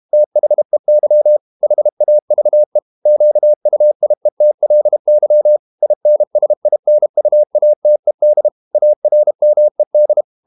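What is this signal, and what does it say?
Morse code, heard as a keyed tone, about 32 wpm, 600 Hz. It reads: THEY HAVE QUIETLY INSINUATED ARMED
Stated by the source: Morse code practice at 32 wpm from CW med Gnister, reading Ken Follett's novel Never in English.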